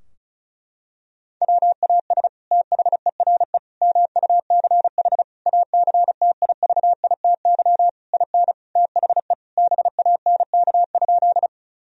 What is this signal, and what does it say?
Morse code, heard as a keyed tone, about 34 wpm, 700 Hz. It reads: WAS THERE MUCH ACTIVITY IN THE BANK?
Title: Sentences from Top 1000 Words - Review Part 4 - 35wpm